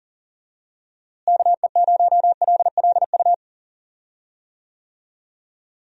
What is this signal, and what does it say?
Morse code, heard as a keyed tone, about 37 wpm, 700 Hz. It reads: KE0LLU
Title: USA Call Signs - 6 Characters - Repeated 3x - 40wpm